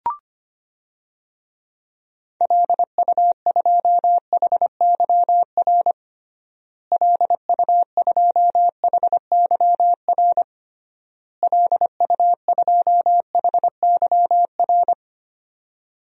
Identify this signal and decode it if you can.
Morse code, heard as a keyed tone, about 25 wpm, 700 Hz. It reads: LU2HYR LU2HYR LU2HYR
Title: DXCC-100 Call Signs - 6 Characters - Repeated 3x - 25wpm